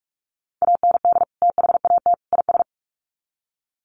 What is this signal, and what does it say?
Morse code, heard as a keyed tone, about 44 wpm, 700 Hz. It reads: AND THAT IS
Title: Sets of 3 Words 45wpm